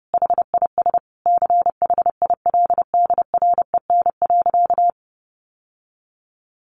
Morse code, heard as a keyed tone, about 30 wpm, 700 Hz. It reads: HIS CHILDREN.
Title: Sentences from Top 1100 Words 30wpm